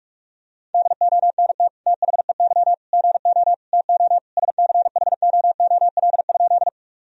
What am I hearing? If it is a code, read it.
Morse code, heard as a keyed tone, about 47 wpm, 700 Hz. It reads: DONT THEY GO TO SCHOOL?
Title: Sentences from Top 1000 Words - Review Part 1 - 45wpm